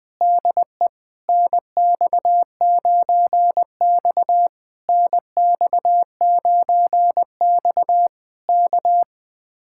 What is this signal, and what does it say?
Morse code, heard as a keyed tone, about 21 wpm, 700 Hz. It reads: DE NX9X NX9X K
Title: Speed-Racing Common QSO Phrases - 20wpm